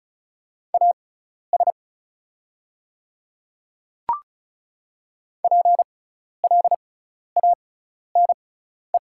Morse code, heard as a keyed tone, about 36 wpm, spaced as for 12 wpm, 700 Hz. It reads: AS PLANE
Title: 6x Character Spacing - Top 500 Words - 35wpm